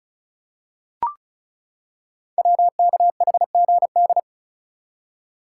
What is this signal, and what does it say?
Morse code, heard as a keyed tone, about 36 wpm, 700 Hz. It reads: WKHGD
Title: Code Groups - 5 Letters - 35wpm